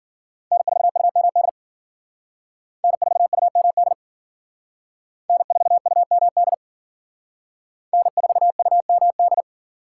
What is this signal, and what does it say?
Morse code, heard as a keyed tone, about 53 wpm, 700 Hz. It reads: N4UMD N4UMD N4UMD N4UMD